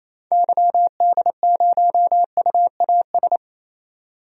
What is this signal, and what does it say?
Morse code, heard as a keyed tone, about 28 wpm, 700 Hz. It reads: YD0UAS
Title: DXCC-100 Call Signs - 6 Characters - 28wpm